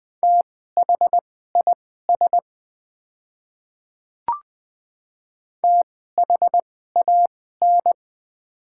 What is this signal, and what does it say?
Morse code, heard as a keyed tone, about 20 wpm, spaced as for 14 wpm, 700 Hz. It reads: THIS THAN
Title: 2x Character Spacing - Top 100 Words - 20wpm